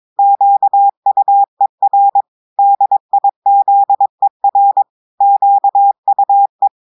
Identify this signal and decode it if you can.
Morse code, heard as a keyed tone, about 22 wpm, 800 Hz. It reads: QUER DIZER QUE
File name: audios / portugais / portugais texte en clair 2 vitesse 22 mots minutes